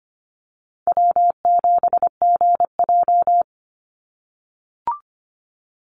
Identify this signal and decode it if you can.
Morse code, heard as a keyed tone, about 25 wpm, 700 Hz. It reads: W7GJ
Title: NA SOTA Chaser Callsigns 25wpm